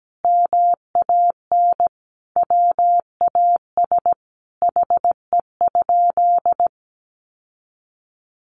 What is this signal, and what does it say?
Morse code, heard as a keyed tone, about 17 wpm, 700 Hz. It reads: MAN WAS HE?
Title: Sentences from Top 1300 Words 17wpm